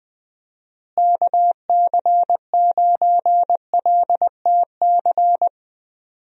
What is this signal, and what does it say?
Morse code, heard as a keyed tone, about 20 wpm, 700 Hz. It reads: KC9LTC
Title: USA Call Signs - 6 Characters - Repeated 2x - 20wpm